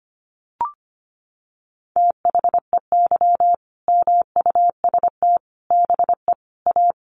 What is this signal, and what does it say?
Morse code, heard as a keyed tone, about 25 wpm, 700 Hz. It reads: THEY MUST BE A
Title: Sentences from Top 1400 Words 25wpm